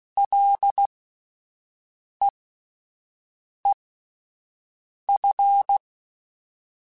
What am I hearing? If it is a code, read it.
Morse code, heard as a keyed tone, about 16 wpm, spaced as for 5 wpm, 800 Hz. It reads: LEEF
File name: Ls / Single L m3s / Single L Prefixe 3x2 Twice @5